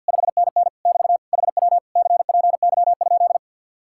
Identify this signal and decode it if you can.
Morse code, heard as a keyed tone, about 58 wpm, 700 Hz. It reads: 5NN <BT> HW CPY?
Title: Speed-Racing Common QSO Phrases - 50wpm